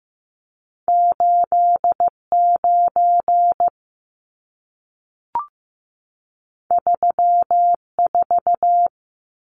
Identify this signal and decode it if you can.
Morse code, heard as a keyed tone, about 15 wpm, 700 Hz. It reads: 89 34
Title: Numbers - 2-Digit 15wpm